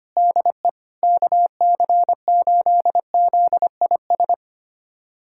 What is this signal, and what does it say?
Morse code, heard as a keyed tone, about 25 wpm, 700 Hz. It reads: DE KC8ZIS
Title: USA Callsign DE Callsign 25wpm